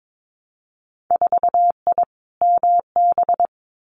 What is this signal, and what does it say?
Morse code, heard as a keyed tone, about 22 wpm, 700 Hz. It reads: 4I MB
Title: Winter Field Day Exchanges Encore 22wpm